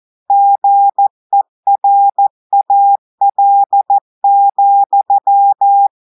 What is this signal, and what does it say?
Morse code, heard as a keyed tone, about 14 wpm, 800 Hz. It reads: GERAL,
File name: audios / portugais / portugais texte en clair 2 vitesse 14 mots minutes